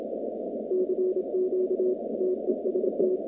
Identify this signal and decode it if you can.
Morse code, heard as a keyed tone, about 26 wpm, 370 Hz, with noise. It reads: CQ TEST
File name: 2019-cw-wpx-kt5j-k5pi